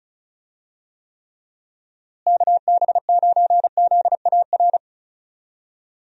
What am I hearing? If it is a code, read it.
Morse code, heard as a keyed tone, about 34 wpm, 700 Hz. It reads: KB9ZAR
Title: USA Call Signs - 6 Characters - Repeated 3x - 35wpm